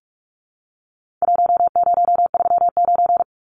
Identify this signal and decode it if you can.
Morse code, heard as a keyed tone, about 44 wpm, 700 Hz. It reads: 1039
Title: Numbers - Contest Style - 4-Digit 45wpm